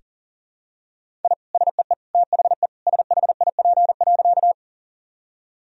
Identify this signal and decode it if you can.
Morse code, heard as a keyed tone, about 40 wpm, 700 Hz. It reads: I SEE THE SHIP.
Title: Sentences from Top 1000 Words - Review Part 2 - 40wpm